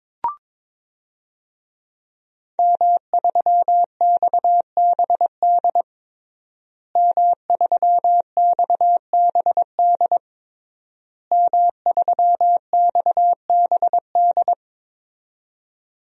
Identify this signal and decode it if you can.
Morse code, heard as a keyed tone, about 22 wpm, 700 Hz. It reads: M3XBD M3XBD M3XBD
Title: DXCC-100 Call Signs - 5 Characters - Repeated 3x - 22wpm